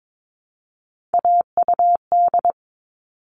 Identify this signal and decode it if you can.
Morse code, heard as a keyed tone, about 22 wpm, 700 Hz. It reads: AUD
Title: All Cut Numbers - 3-Digit 22wpm